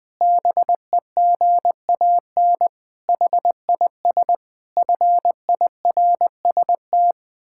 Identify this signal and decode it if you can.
Morse code, heard as a keyed tone, about 20 wpm, 700 Hz. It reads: BEGAN HIS FIRST
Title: Pangram Challenge v2 - 20wpm